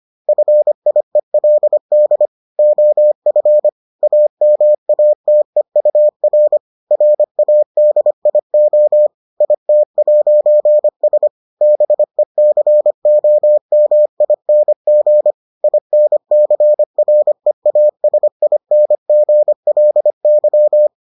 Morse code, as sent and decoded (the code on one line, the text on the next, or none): ..-. .. .-.. -.. / --- ..-. / .- -- .- - . ..- .-. / .-. .- -.. .. --- / .. - .----. ... / -... -.-. --- -- .. -. --. / .. -. -.-. .-. .- ... .. -. --. .-.. -.--
FIELD OF AMATEUR RADIO IT'S BECOMING INCREASINGLY